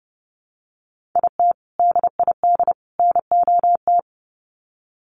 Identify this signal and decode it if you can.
Morse code, heard as a keyed tone, about 30 wpm, 700 Hz. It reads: IT DID NOT
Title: Sets of 3 Words 30wpm